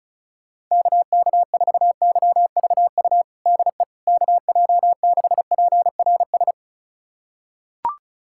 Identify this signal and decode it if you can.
Morse code, heard as a keyed tone, about 34 wpm, 700 Hz. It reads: KK4YVU DE KJ6PRS
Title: USA Callsign DE Callsign 35wpm